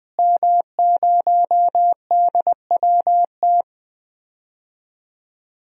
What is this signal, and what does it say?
Morse code, heard as a keyed tone, about 20 wpm, 700 Hz. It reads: M0DWT
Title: DXCC-100 Call Signs - 5 Characters - Repeated 3x - 20wpm